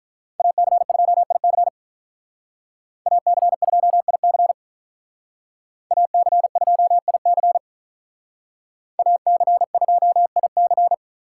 Morse code, heard as a keyed tone, about 46 wpm, 700 Hz. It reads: AC2IC AC2IC AC2IC AC2IC